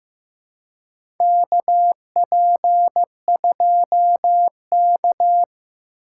Morse code, heard as a keyed tone, about 16 wpm, 700 Hz. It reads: KP2K
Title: USA Call Signs - 4 Characters - 4x Character Spacing - 15wpm